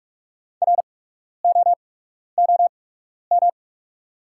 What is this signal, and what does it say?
Morse code, heard as a keyed tone, about 45 wpm, spaced as for 12 wpm, 690 Hz. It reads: ROOM
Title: 8x Character Spacing - Top 500 Words - 45wpm